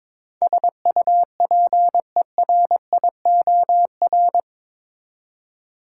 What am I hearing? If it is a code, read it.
Morse code, heard as a keyed tone, about 22 wpm, 700 Hz. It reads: SUPERIOR
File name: practice / 22wpm / Eight Letter Words - 22wpm